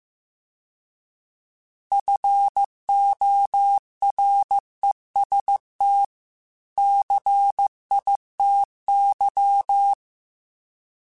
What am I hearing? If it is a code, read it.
Morse code, heard as a keyed tone, about 15 wpm, 800 Hz. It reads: FOREST CITY